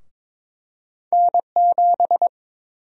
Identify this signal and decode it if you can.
Morse code, heard as a keyed tone, about 22 wpm, 700 Hz. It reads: N7
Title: USA Call Sign Prefixes - 22wpm